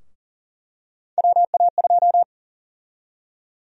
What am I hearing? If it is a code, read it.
Morse code, heard as a keyed tone, about 40 wpm, 700 Hz. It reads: WA2
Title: USA Call Sign Prefixes - Repeated 2x - 40wpm